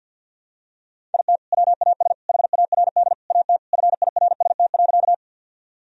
Morse code, heard as a keyed tone, about 52 wpm, 700 Hz. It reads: IT WAS HARD AT FIRST.
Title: Sentences from Top 1000 Words - Review Part 1 - 50wpm